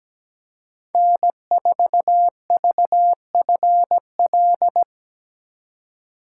Morse code, heard as a keyed tone, about 17 wpm, 700 Hz. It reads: N4VFL
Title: USA Call Signs - 5 Characters - 4x Character Spacing - 17wpm